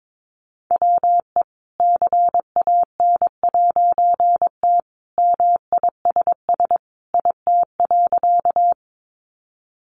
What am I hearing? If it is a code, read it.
Morse code, heard as a keyed tone, about 22 wpm, 700 Hz. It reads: WE CAN'T MISS IT.
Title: A Sound Of Thunder 22wpm